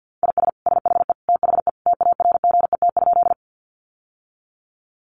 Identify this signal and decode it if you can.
Morse code, heard as a keyed tone, about 49 wpm, 700 Hz. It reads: IS SHE THE TARGET?
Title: Sentences from Top 1200 Words 50wpm